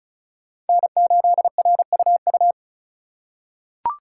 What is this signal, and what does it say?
Morse code, heard as a keyed tone, about 34 wpm, 700 Hz. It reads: N8RUU
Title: USA Call Signs - 5 Characters - 3x Character Spacing - 35wpm